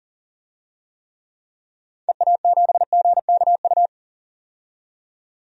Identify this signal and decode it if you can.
Morse code, heard as a keyed tone, about 40 wpm, 700 Hz. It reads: EA7GKU
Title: DXCC-100 Call Signs - 6 Characters - 40wpm